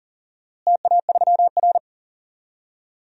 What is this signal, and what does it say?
Morse code, heard as a keyed tone, about 40 wpm, 700 Hz. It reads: TA3R
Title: DXCC-100 Call Signs - 4 Characters - Repeated 3x - 40wpm